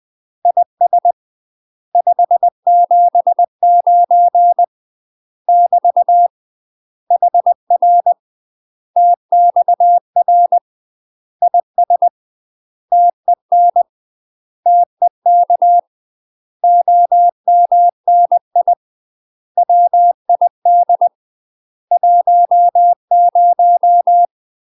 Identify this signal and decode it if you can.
Morse code, heard as a keyed tone, about 20 wpm, 700 Hz. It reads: IS 579 <BT> HR TXR IS TEN TEK OMNI WID 10
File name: UK MASTER QSOs TRAINING/20WPM